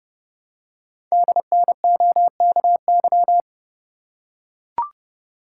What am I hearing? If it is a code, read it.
Morse code, heard as a keyed tone, about 30 wpm, 690 Hz. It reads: DNOKY
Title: Code Groups - 5 Letters - 30wpm